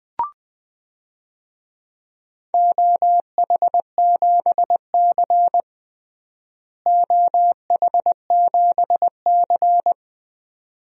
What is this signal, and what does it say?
Morse code, heard as a keyed tone, about 21 wpm, 700 Hz. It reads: OH7C OH7C
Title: DXCC-100 Call Signs - 4 Characters - Repeated 2x - 20wpm